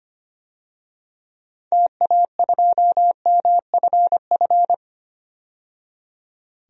Morse code, heard as a keyed tone, about 25 wpm, 700 Hz. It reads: TA2MFF
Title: DXCC-100 Call Signs - 6 Characters - 25wpm